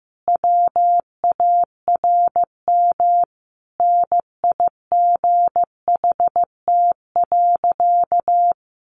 Morse code, heard as a keyed tone, about 15 wpm, 700 Hz. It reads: WARM NIGHT.
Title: Sentences from Top 500 Words 15wpm